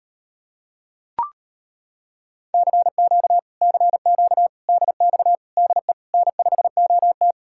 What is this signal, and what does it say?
Morse code, heard as a keyed tone, about 33 wpm, 700 Hz. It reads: CQ CQ DX DE N5OT